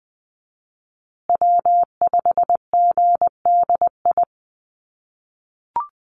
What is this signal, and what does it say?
Morse code, heard as a keyed tone, about 20 wpm, 700 Hz. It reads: W5GDI
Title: USA Call Signs Repeated 2x 20wpm